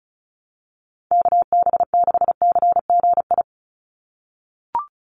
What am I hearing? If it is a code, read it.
Morse code, heard as a keyed tone, about 34 wpm, 700 Hz. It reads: KB6CGI
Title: USA Call Signs 35wpm